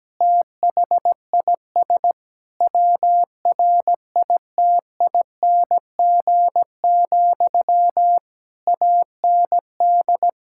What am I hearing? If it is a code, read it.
Morse code, heard as a keyed tone, about 17 wpm, 700 Hz. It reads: THIS WRITING, AND
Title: Sentences from Top 1000 Words - Review Part 6 - 17wpm